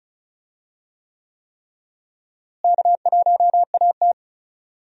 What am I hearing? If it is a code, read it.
Morse code, heard as a keyed tone, about 36 wpm, 700 Hz. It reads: K1AT
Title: USA Call Signs - 4 Characters - Repeated 3x - 35wpm